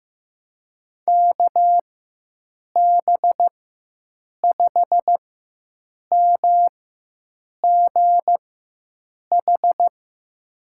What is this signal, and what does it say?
Morse code, heard as a keyed tone, about 15 wpm, spaced as for 7 wpm, 700 Hz. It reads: KB5MGH
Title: USA Call Signs - 6 Characters - 4x Character Spacing - 15wpm